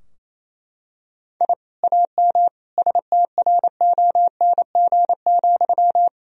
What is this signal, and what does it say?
Morse code, heard as a keyed tone, about 28 wpm, 700 Hz. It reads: I AM STRONG,
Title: Sentences from Top 1000 Words - Review Part 3 - 28wpm